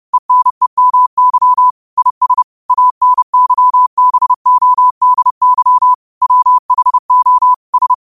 Morse code, heard as a keyed tone, about 30 wpm, 1000 Hz. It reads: ENEMY IS ANYBODY WHO S
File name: qotdfiles / 30wpm